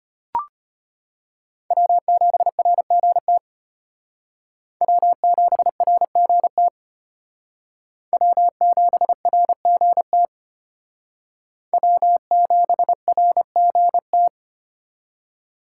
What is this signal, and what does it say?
Morse code, heard as a keyed tone, about 35 wpm, 700 Hz. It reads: W7RGT W7RGT W7RGT W7RGT